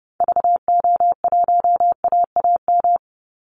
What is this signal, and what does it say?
Morse code, heard as a keyed tone, about 30 wpm, 700 Hz. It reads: VO1AAM
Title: Call Signs Repeated 3x - 30wpm